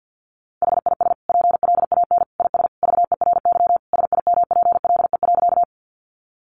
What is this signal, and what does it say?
Morse code, heard as a keyed tone, about 50 wpm, 700 Hz. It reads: HIS PLAN IS VERY SIMPLE.
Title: Sentences from Top 500 Words 50wpm